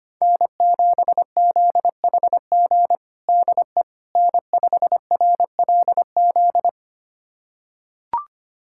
Morse code, heard as a keyed tone, about 25 wpm, 700 Hz. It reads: N7ZHG DE N5RLZ
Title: USA Callsign DE Callsign - 5 Characters - 25wpm